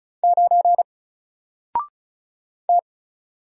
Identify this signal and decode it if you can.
Morse code, heard as a keyed tone, about 35 wpm, 700 Hz. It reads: 9 T
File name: Single-Letter-Number-35wpm